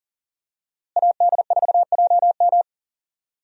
Morse code, heard as a keyed tone, about 40 wpm, 700 Hz. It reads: AD4JM
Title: USA Call Signs - 5 Characters - Repeated 3x - 40wpm